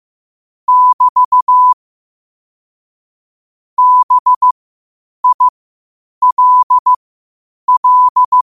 Morse code, heard as a keyed tone, about 15 wpm, spaced as for 8 wpm, 1000 Hz. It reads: <BT> BILL
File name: qotdfiles / 8wpm